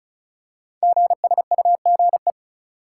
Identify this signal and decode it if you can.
Morse code, heard as a keyed tone, about 36 wpm, 690 Hz. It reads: GSUGE
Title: Code Groups - 5 Letters - 35wpm